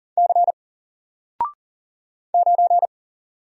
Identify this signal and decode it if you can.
Morse code, heard as a keyed tone, about 40 wpm, 700 Hz. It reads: C 9
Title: Single Letter-Number 40wpm